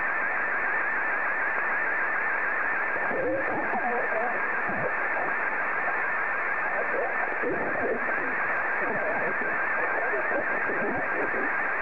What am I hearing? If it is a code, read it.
Two-tone teleprinter signal, about 170 Hz shift, with speech.